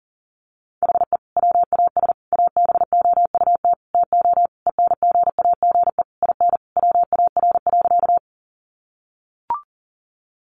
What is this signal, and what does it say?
Morse code, heard as a keyed tone, about 40 wpm, 700 Hz. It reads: HE WAS ABOUT TO ENGAGE IN WAR.
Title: Sentences from Top 1500 Words 40wpm